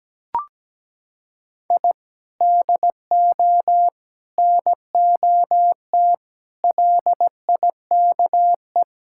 Morse code, heard as a keyed tone, about 17 wpm, 700 Hz. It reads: I DO NOT LIKE
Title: Sentences from Top 1000 Words - Review Part 6 - 17wpm